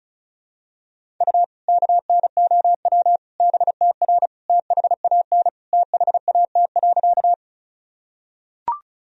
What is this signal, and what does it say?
Morse code, heard as a keyed tone, about 34 wpm, 700 Hz. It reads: U KNOW BTR THAN THAT.
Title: Sentences from Top 400 Words and Abbreviations 35wpm